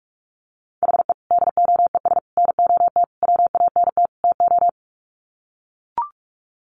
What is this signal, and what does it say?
Morse code, heard as a keyed tone, about 44 wpm, 700 Hz. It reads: HE DOES NOT WANT TO
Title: Sets of 5 Words 45wpm